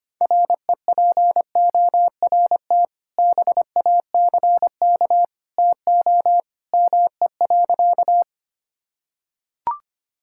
Morse code, heard as a keyed tone, about 25 wpm, 700 Hz. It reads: REPORT BACK TO ME.
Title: Sentences from Top 1000 Words - Review Part 6 - 25wpm